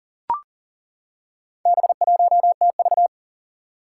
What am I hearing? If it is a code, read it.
Morse code, heard as a keyed tone, about 42 wpm, 700 Hz. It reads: B1TV
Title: Four Challenging Letter-Number Combinations 40wpm